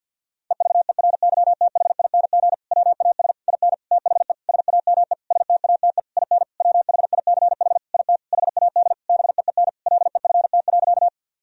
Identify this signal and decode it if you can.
Morse code, heard as a keyed tone, about 50 wpm, 700 Hz. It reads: EVERYTHING WAS IN THE SAME STATE IN WHICH IT HAD BEEN LEFT.
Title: Sentences from Top 1000 Words - Review Part 2 - 50wpm